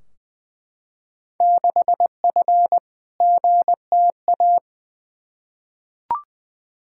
Morse code, heard as a keyed tone, about 20 wpm, 700 Hz. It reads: 6F GTA
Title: Field Day Exchanges Encore - 20wpm